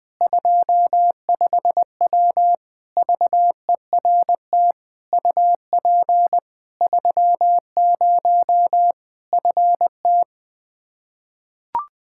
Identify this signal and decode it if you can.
Morse code, heard as a keyed tone, about 21 wpm, 700 Hz. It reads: 25W VERT UP 30 FT
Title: Practice QSO - Exchange 2 - Info - 20wpm